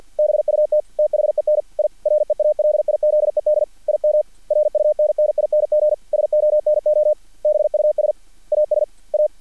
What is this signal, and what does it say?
Morse code, heard as a keyed tone, about 50 wpm, 600 Hz. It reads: BUT THEN I REALIZED IM RUNNING SONOBUS AS A
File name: archived / CRSnet